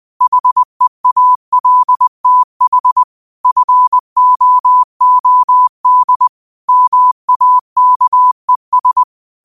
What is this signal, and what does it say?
Morse code, heard as a keyed tone, about 20 wpm, 1000 Hz. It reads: HEALTH FOOD MAKES
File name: qotdfiles / 20wpm